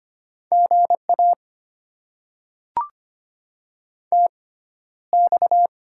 Morse code, heard as a keyed tone, about 25 wpm, spaced as for 13 wpm, 700 Hz. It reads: GA TX
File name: Top 90 QSO Elements - 6x Character Spacing - 25wpm